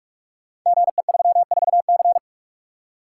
Morse code, heard as a keyed tone, about 44 wpm, 700 Hz. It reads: GE34C
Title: Code Groups - 5 Letter:Numbers - 45wpm